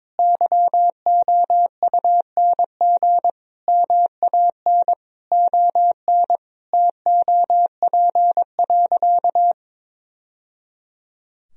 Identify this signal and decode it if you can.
Morse code, heard as a keyed tone, about 21 wpm, 700 Hz. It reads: YOUNG MAN ON TOP.
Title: Sentences from Top 1000 Words - Review Part 2 - 22wpm